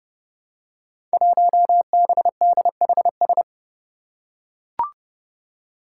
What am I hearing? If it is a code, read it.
Morse code, heard as a keyed tone, about 30 wpm, 700 Hz. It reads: 1BDHS
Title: Five Challenging Letter-Number Combinations 30wpm